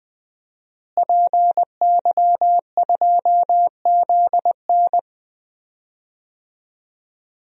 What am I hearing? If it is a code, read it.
Morse code, heard as a keyed tone, about 20 wpm, 700 Hz. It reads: PY2ZN
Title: DXCC-100 Call Signs - 5 Characters - Repeated 3x - 20wpm